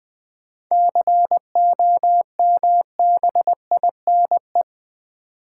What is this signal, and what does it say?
Morse code, heard as a keyed tone, about 21 wpm, 700 Hz. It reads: COMBINE
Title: Sentences from Top 1600 Words - New Word Drill - 20wpm